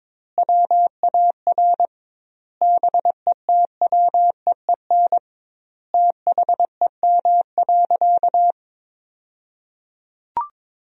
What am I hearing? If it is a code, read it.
Morse code, heard as a keyed tone, about 23 wpm, 700 Hz. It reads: WAR BETWEEN THEM.
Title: 2x Word-Spacing - Sentences from Top 400 Words - 22wpm